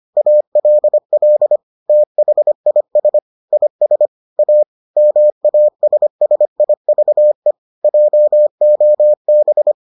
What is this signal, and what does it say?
Morse code, heard as a keyed tone, about 25 wpm, 600 Hz. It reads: ALL THIS IS A MASSIVE JOB